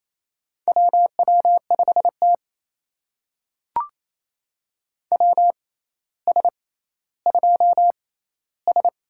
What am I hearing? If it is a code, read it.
Morse code, heard as a keyed tone, about 28 wpm, spaced as for 14 wpm, 700 Hz. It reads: WW5T WS2S
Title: USA Call Signs - 4 Characters - 6x Character Spacing - 28wpm